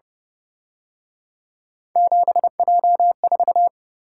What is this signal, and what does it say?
Morse code, heard as a keyed tone, about 30 wpm, 700 Hz. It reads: 7J4
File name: Speed-Racing - Three Letter-Number Combinations - 25wpm